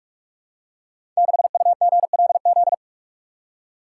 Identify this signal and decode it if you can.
Morse code, heard as a keyed tone, about 44 wpm, 690 Hz. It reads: 6UGLZ
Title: Code Groups - 5 Letter:Numbers - 45wpm